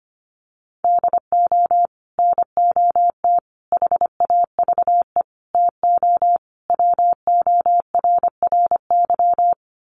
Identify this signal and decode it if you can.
Morse code, heard as a keyed tone, about 25 wpm, 700 Hz. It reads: DO NOT HAVE TO WORRY
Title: Sets of 5 Words 25wpm